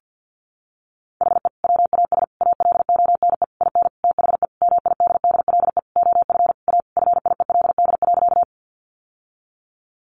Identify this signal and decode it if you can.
Morse code, heard as a keyed tone, about 50 wpm, 700 Hz. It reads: HE WAS ALONE IN THE MIDDLE OF A FIELD.
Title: Sentences from Top 1000 Words 50wpm